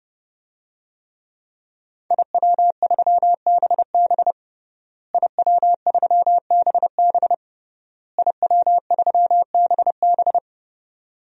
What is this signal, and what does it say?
Morse code, heard as a keyed tone, about 32 wpm, 700 Hz. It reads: IW3BB IW3BB IW3BB
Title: DXCC-100 Call Signs - 5 Characters - Repeated 3x - 30wpm